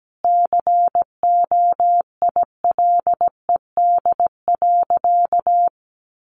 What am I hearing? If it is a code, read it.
Morse code, heard as a keyed tone, about 17 wpm, 700 Hz. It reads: COILED.
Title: A Sound Of Thunder 17wpm